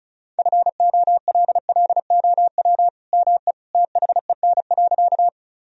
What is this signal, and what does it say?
Morse code, heard as a keyed tone, about 36 wpm, 700 Hz. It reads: FOLLOW ME THEN.
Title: Sentences from Top 1000 Words - Review Part 1 - 35wpm